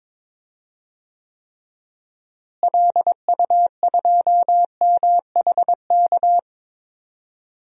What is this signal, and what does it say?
Morse code, heard as a keyed tone, about 22 wpm, 700 Hz. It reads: LU2MHK
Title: DXCC-100 Call Signs - 6 Characters - Repeated 3x - 22wpm